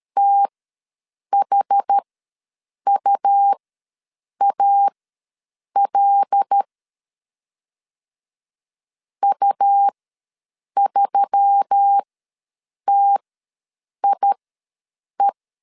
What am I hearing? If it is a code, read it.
Morse code, heard as a keyed tone, about 13 wpm, spaced as for 7 wpm, 800 Hz. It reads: THUAL U3TIE
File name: LEZIONI / 5 LBVC / 5 L ws 12wpm 2958